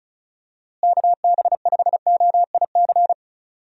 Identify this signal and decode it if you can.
Morse code, heard as a keyed tone, about 36 wpm, 700 Hz. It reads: KB5OIC
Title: USA Call Signs - 6 Characters - Repeated 2x - 35wpm